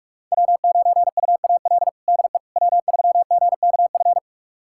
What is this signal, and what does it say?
Morse code, heard as a keyed tone, about 42 wpm, 700 Hz. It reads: W9UAL DE W3GKF
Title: USA Callsign DE Callsign - 5 Characters - 45wpm